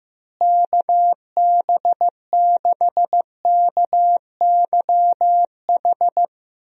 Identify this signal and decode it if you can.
Morse code, heard as a keyed tone, about 15 wpm, 700 Hz. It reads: KB6KYH
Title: USA Call Signs - 6 Characters - Repeated 2x - 15wpm